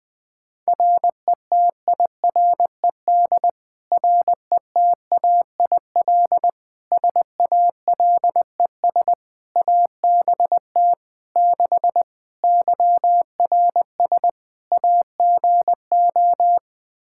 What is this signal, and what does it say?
Morse code, heard as a keyed tone, about 20 wpm, 700 Hz. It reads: RETIRED RETAIL SALES ABT 6 YRS AGO
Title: Practice QSO - Exchange 3 - More Info - 20wpm